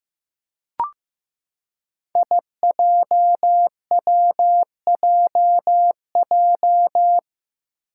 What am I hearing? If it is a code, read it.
Morse code, heard as a keyed tone, about 15 wpm, 700 Hz. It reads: IJWJJ
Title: Five Challenging Letter-Number Combinations 15wpm